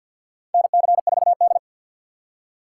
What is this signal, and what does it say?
Morse code, heard as a keyed tone, about 50 wpm, 700 Hz. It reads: NC4D
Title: USA Call Signs - 6x Character Spacing - 50wpm